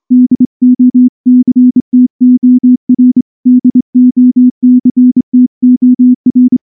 Morse code, heard as a keyed tone, about 25 wpm, 250 Hz. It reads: DOCTOR DOCTOR
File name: morse / morse-training-25WPM